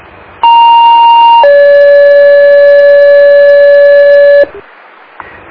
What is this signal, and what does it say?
Two-tone page, heard as a one-second tone, then a three-second tone, about 930 then 570 Hz. It is heard from 0.4 to 4.4 s.